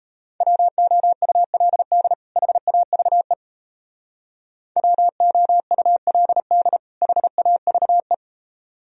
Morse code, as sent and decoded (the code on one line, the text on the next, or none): .-- --- ..- .-.. -.. / .... .- ...- . / .-- --- ..- .-.. -.. / .... .- ...- .
WOULD HAVE WOULD HAVE